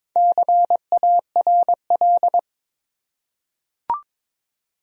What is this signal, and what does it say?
Morse code, heard as a keyed tone, about 22 wpm, 700 Hz. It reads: CARL
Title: USA Common Names - 22wpm